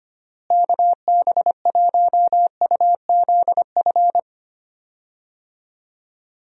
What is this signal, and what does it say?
Morse code, heard as a keyed tone, about 25 wpm, 700 Hz. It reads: KB1UZF
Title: USA Call Signs - 6 Characters - Repeated 2x - 25wpm